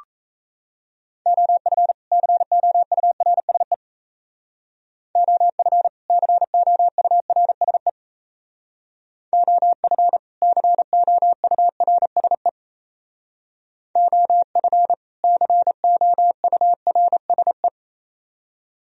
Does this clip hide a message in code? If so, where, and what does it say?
Yes: Morse code from 1.3 to 17.7 s, heard as a keyed tone, about 36 wpm, 700 Hz. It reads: OF COURSE OF COURSE OF COURSE OF COURSE